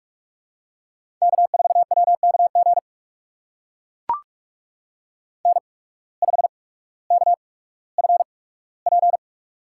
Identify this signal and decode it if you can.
Morse code, heard as a keyed tone, about 42 wpm, spaced as for 19 wpm, 700 Hz. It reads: K4WKG N5KFP
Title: USA Call Signs - 5 Characters - 8x Character Spacing - 45wpm